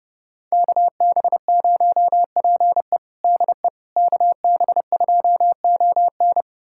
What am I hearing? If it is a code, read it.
Morse code, heard as a keyed tone, about 32 wpm, 700 Hz. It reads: KB0PE DE KB2ON
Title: USA Callsign DE Callsign - 5 Characters - 30wpm